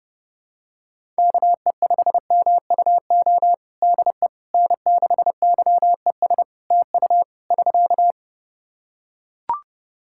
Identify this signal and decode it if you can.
Morse code, heard as a keyed tone, about 30 wpm, 700 Hz. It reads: KE5MUO DE N6YES TU <SK>
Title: K1USN SST QSO Practice 30wpm